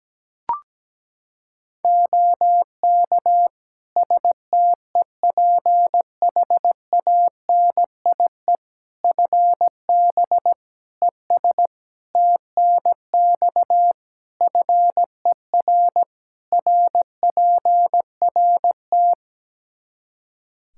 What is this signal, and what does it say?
Morse code, heard as a keyed tone, about 17 wpm, 700 Hz. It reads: OK STEPHANIE FB ES TNX FER RPRT
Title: Practice QSO - Exchange 2 - Info - 17wpm